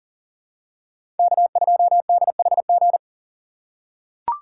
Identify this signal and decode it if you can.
Morse code, heard as a keyed tone, about 40 wpm, 700 Hz. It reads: K2DHG
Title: USA Call Signs - 5 Characters - 3x Character Spacing - 40wpm